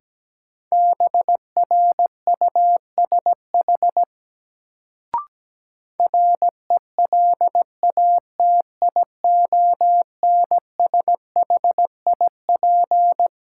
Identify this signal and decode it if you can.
Morse code, heard as a keyed tone, about 17 wpm, 700 Hz. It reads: BRUSH RELATIONSHIP
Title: Sentences from Top 1100 Words - New Word Drill - 17wpm